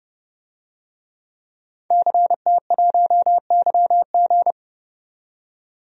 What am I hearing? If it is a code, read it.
Morse code, heard as a keyed tone, about 30 wpm, 700 Hz. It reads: CT1YG
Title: DXCC-100 Call Signs - 5 Characters - 30wpm